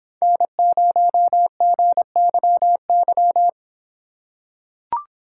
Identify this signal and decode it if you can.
Morse code, heard as a keyed tone, about 26 wpm, 700 Hz. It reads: N0GYY